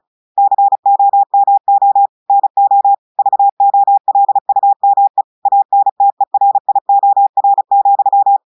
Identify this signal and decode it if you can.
Morse code, heard as a keyed tone, about 36 wpm, 800 Hz. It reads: COMO NO VOLUME ANTERIOR,
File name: audios / portugais / portugais texte en clair 1 vitesse 35 mots minutes